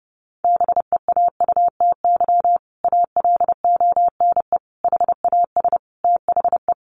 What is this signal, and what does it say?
Morse code, heard as a keyed tone, about 30 wpm, 700 Hz. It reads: BEAUTY ALONE HAS THE
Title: Sentences from Top 1200 Words 30wpm